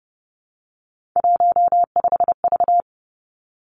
Morse code, heard as a keyed tone, about 30 wpm, 700 Hz. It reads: 15V
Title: Three Challenging Letter-Number Combinations 30wpm